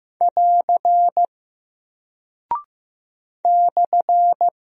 Morse code, heard as a keyed tone, about 15 wpm, 700 Hz. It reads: <AR> /
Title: Single Character (Letter:Number:Punctuation:Prosign) 15wpm